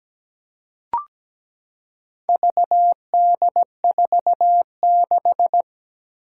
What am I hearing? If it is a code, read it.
Morse code, heard as a keyed tone, about 17 wpm, 700 Hz. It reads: VD46
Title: Four Challenging Letter-Number Combinations 17wpm